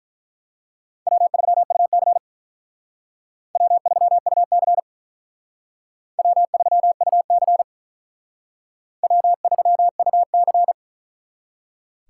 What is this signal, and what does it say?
Morse code, heard as a keyed tone, about 42 wpm, 700 Hz. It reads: W3UC W3UC W3UC W3UC
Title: Speed Racing - USA Call Signs 35wpm